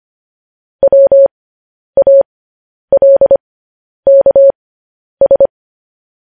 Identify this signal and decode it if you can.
Morse code, heard as a keyed tone, about 25 wpm, spaced as for 10 wpm, 550 Hz. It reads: WALKS
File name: podcast / News headlines-1x-25x10